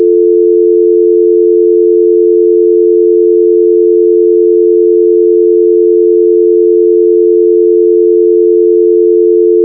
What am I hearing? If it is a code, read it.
Dial tone (North America), 350 + 440 Hz.
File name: HTC / Ozone / HTC DialTone